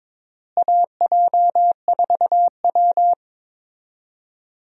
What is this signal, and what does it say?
Morse code, heard as a keyed tone, about 21 wpm, 700 Hz. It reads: AJ4W
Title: USA Call Signs - 4 Characters - Repeated 2x - 22wpm